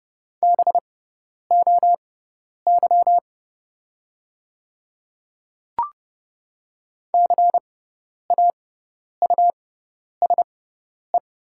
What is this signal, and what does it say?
Morse code, heard as a keyed tone, about 30 wpm, spaced as for 10 wpm, 690 Hz. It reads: BOY CAUSE
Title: 6x Character Spacing - Top 300 Words - 30wpm